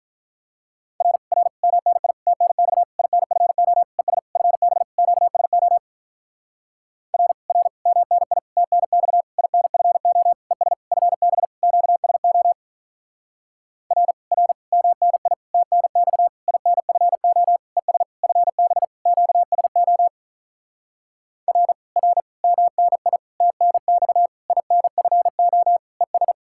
Morse code, heard as a keyed tone, about 47 wpm, 700 Hz. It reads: R R MNI TNX INFO ES FB QSO R R MNI TNX INFO ES FB QSO R R MNI TNX INFO ES FB QSO R R MNI TNX INFO ES